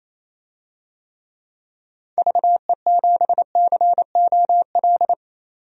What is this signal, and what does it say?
Morse code, heard as a keyed tone, about 28 wpm, 700 Hz. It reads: VE7COL
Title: DXCC-100 Call Signs - 6 Characters - 28wpm